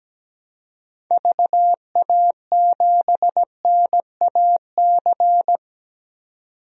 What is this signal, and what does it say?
Morse code, heard as a keyed tone, about 17 wpm, 700 Hz. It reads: VA7NAC
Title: DXCC-100 Call Signs - 6 Characters - 17wpm